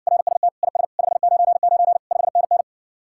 Morse code, heard as a keyed tone, about 61 wpm, 700 Hz. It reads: RST IS 599 5NN